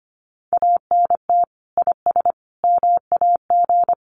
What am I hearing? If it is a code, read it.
Morse code, heard as a keyed tone, about 25 wpm, 700 Hz. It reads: ANT IS MAG
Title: Common QSO Phrases 25wpm